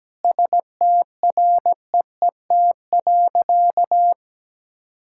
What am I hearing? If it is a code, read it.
Morse code, heard as a keyed tone, about 17 wpm, 700 Hz. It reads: STREET.
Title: Sentences from Top 1000 Words - Review Part 5 - 17wpm